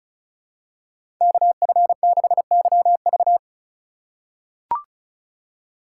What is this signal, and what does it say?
Morse code, heard as a keyed tone, about 36 wpm, 700 Hz. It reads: KF6YV
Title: USA Call Signs - 5 Characters - 3x Character Spacing - 35wpm